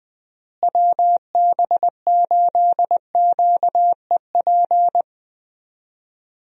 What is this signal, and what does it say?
Morse code, heard as a keyed tone, about 19 wpm, 700 Hz. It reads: WB8QEP
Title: USA Call Signs - 6 Characters - Repeated 2x - 20wpm